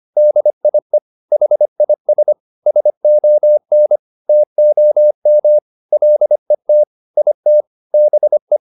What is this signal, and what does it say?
Morse code, heard as a keyed tone, about 25 wpm, 600 Hz. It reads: DIE HIS SON TOM LET IT BE